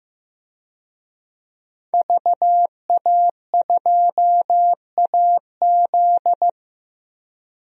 Morse code, heard as a keyed tone, about 16 wpm, 700 Hz. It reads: VA2AZ
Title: DXCC-100 Call Signs - 5 Characters - Repeated 3x - 15wpm